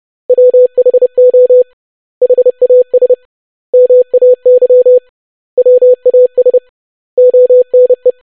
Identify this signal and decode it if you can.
Morse code, heard as a keyed tone, about 30 wpm, 500 Hz. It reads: WHO HAS MAY WAS ONE
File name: basic100-25wpm